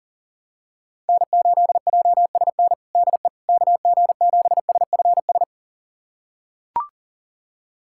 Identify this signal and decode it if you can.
Morse code, heard as a keyed tone, about 37 wpm, 700 Hz. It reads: N8JSN DE KG7SFS
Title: USA Callsign DE Callsign 40wpm